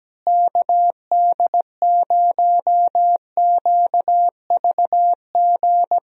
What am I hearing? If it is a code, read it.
Morse code, heard as a keyed tone, about 17 wpm, 700 Hz. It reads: KD0QVG
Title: USA Call Signs - 6 Characters - Repeated 3x - 17wpm